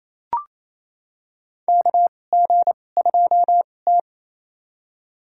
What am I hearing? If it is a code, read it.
Morse code, heard as a keyed tone, about 28 wpm, spaced as for 20 wpm, 700 Hz. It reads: KG2T
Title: USA Call Signs - 4 Characters - 2x Character Spacing - 28wpm